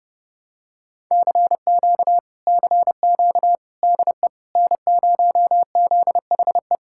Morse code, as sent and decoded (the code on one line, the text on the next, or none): -.-. --.- / -.-. --.- / -.. / -. ----- --.. .... .
CQ CQ DE N0ZHE